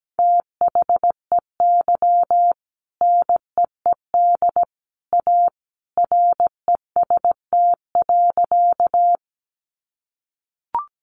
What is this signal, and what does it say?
Morse code, heard as a keyed tone, about 17 wpm, 700 Hz. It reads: THEY NEED A REST.
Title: Sentences from Top 500 Words 17wpm